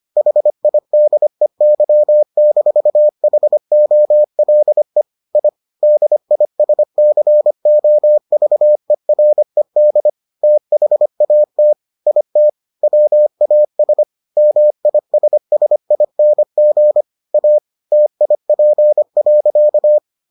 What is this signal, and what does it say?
Morse code, heard as a keyed tone, about 25 wpm, 600 Hz. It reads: HIDEY-HOLE I DISCOVERED THAT IT WAS MISSING A TIP.